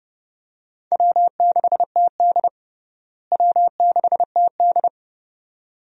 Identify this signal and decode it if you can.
Morse code, heard as a keyed tone, about 32 wpm, 700 Hz. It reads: W6TD W6TD
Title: USA Call Signs - 4 Characters - Repeated 2x - 30wpm